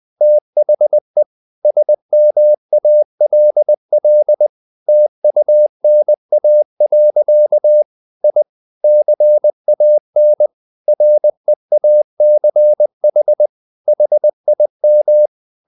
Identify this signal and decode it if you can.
Morse code, heard as a keyed tone, about 20 wpm, 600 Hz. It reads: THE SMALL TUNA. I CAN REACH HIM
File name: the old man and the sea / the old man and the sea part 1 20wpm